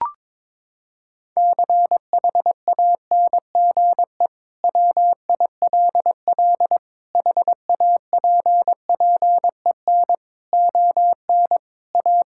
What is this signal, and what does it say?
Morse code, heard as a keyed tone, about 22 wpm, 700 Hz. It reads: CHANGE WILL HAPPEN ON A
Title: Sentences from Top 1000 Words - Review Part 5 - 22wpm